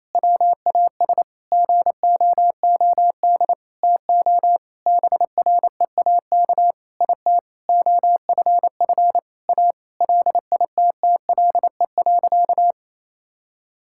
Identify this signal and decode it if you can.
Morse code, heard as a keyed tone, about 27 wpm, 700 Hz. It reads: WAS GOOD TO BREAK IT OFF A LITTLE.
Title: Sentences from Top 1000 Words - Review Part 6 - 28wpm